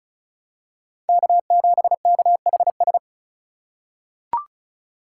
Morse code, heard as a keyed tone, about 36 wpm, 700 Hz. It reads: K7KHS
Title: USA Call Signs - 5 Characters - 2x Character Spacing - 35wpm